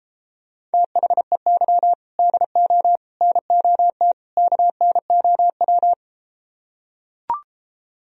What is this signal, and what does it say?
Morse code, heard as a keyed tone, about 34 wpm, 700 Hz. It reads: THEY DO NOT KNOW